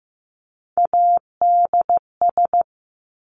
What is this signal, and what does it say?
Morse code, heard as a keyed tone, about 15 wpm, 700 Hz. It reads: ADS
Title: Abbreviations 15wpm